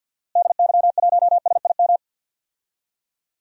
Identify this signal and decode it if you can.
Morse code, heard as a keyed tone, about 52 wpm, 700 Hz. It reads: DY1SIM
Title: DXCC-100 Call Signs - 6 Characters - Repeated 3x - 50wpm